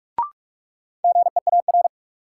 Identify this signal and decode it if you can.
Morse code, heard as a keyed tone, about 46 wpm, 700 Hz. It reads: GEAR
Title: Sentences from Top 1400 Words - New Word Drill Alt - 45wpm